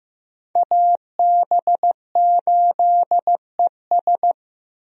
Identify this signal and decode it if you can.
Morse code, heard as a keyed tone, about 15 wpm, 700 Hz. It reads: AB8ES
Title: USA Call Signs - 5 Characters - Repeated 2x - 15wpm